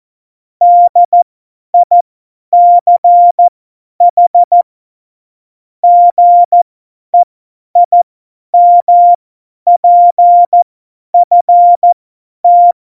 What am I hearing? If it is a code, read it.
Morse code, heard as a keyed tone, about 14 wpm, spaced as for 10 wpm, 700 Hz. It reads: DICH GEIMPFT
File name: Morsekurs / Witze / Witz 05. Dezember 2017, 50 F70